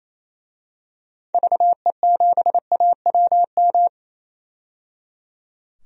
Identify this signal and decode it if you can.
Morse code, heard as a keyed tone, about 28 wpm, 700 Hz. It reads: VE7AWM
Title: DXCC-100 Call Signs - 6 Characters - 28wpm